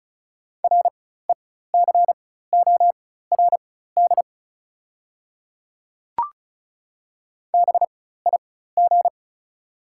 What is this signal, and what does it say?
Morse code, heard as a keyed tone, about 36 wpm, spaced as for 16 wpm, 700 Hz. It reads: RECORD BIG
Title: 4x Character Spacing - Top 500 Words - 35wpm